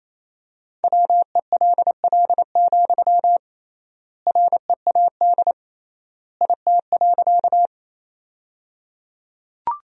Morse code, heard as a keyed tone, about 28 wpm, 700 Hz. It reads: WELL, READ IT.
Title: 3x Word-Spacing - Sentences from Top 200 Words - 28wpm